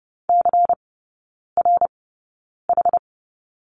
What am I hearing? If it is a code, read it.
Morse code, heard as a keyed tone, about 30 wpm, spaced as for 9 wpm, 700 Hz. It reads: CRH
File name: Lesson 18e - Letter m - Three Characters 30wpm